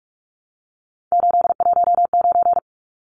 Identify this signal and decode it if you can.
Morse code, heard as a keyed tone, about 44 wpm, 700 Hz. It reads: 819
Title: Numbers - 3-Digit 45wpm